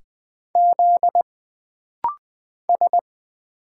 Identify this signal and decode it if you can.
Morse code, heard as a keyed tone, about 20 wpm, 700 Hz. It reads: Z S
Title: Single Letters - Rapid-Fire 20wpm